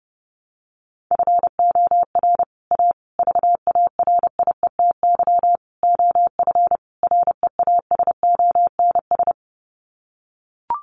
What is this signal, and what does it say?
Morse code, heard as a keyed tone, about 30 wpm, 700 Hz. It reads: FOR A VARIETY OF REASONS
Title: Sets of 5 Words Encore 30wpm